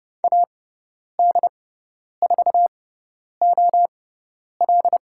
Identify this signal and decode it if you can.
Morse code, heard as a keyed tone, about 30 wpm, spaced as for 10 wpm, 700 Hz. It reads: AD4OL
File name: Lesson 17g - Letter u - Call Signs 30wpm